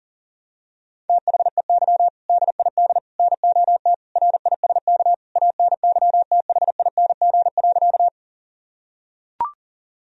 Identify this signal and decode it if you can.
Morse code, heard as a keyed tone, about 37 wpm, 700 Hz. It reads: THEY DID NOT RISK ANYTHING.
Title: Sentences from Top 1000 Words - Review Part 4 - 40wpm